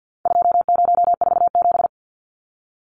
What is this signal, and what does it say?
Morse code, heard as a keyed tone, about 50 wpm, 700 Hz. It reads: 2047
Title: Numbers - 4-Digit 50wpm